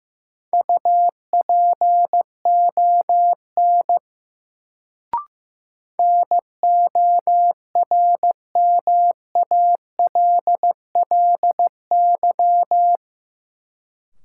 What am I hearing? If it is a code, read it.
Morse code, heard as a keyed tone, about 15 wpm, 700 Hz. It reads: UPON NORMALLY
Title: Sentences from Top 1000 Words - Review Part 6 - New Word Drill - 15wpm